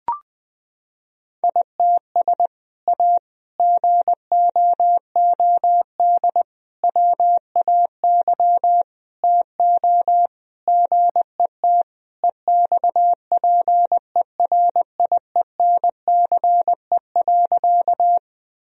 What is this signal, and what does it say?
Morse code, heard as a keyed tone, about 20 wpm, 700 Hz. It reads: ITS A GOOD WAY TO GET EXPERIENCE.
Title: Sentences from Top 1000 Words - Review Part 3 - 20wpm